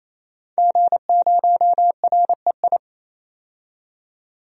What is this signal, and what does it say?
Morse code, heard as a keyed tone, about 28 wpm, 700 Hz. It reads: G0REI
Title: DXCC-100 Call Signs - 5 Characters - 28wpm